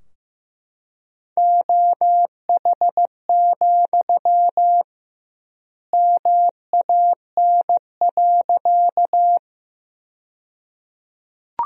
Morse code, heard as a keyed tone, about 15 wpm, 700 Hz. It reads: OH, MAN.
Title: 2x Word-Spacing - Sentences from Top 500 Words - 15wpm